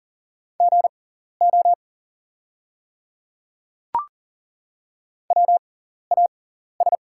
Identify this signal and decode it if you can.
Morse code, heard as a keyed tone, about 40 wpm, spaced as for 14 wpm, 700 Hz. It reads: GO WAS
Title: 6x Character Spacing - Top 100 Words - 40wpm